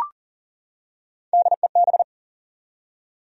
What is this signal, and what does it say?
Morse code, heard as a keyed tone, about 40 wpm, 700 Hz. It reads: DEB
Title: USA Common Names - Max 3 Letters - 40wpm